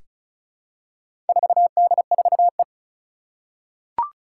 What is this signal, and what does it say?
Morse code, heard as a keyed tone, about 36 wpm, 700 Hz. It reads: 4D4E
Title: Four Challenging Letter-Number Combinations 35wpm